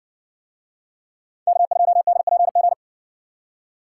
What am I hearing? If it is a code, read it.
Morse code, heard as a keyed tone, about 56 wpm, 700 Hz. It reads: K2DPG